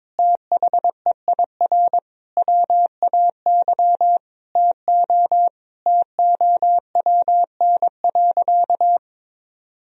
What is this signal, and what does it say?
Morse code, heard as a keyed tone, about 22 wpm, 700 Hz. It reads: THEIR WAY TO TOWN.
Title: Sentences from Top 1000 Words - Review Part 1 - 22wpm